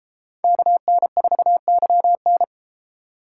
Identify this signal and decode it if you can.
Morse code, heard as a keyed tone, about 31 wpm, 700 Hz. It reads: KN4YN